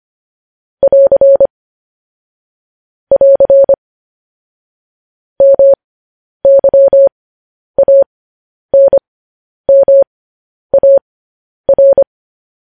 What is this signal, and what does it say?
Morse code, heard as a keyed tone, about 25 wpm, spaced as for 10 wpm, 550 Hz. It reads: <AR> <AR> MYANMAR
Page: news-headlines-in-morse-code-at-10-wpm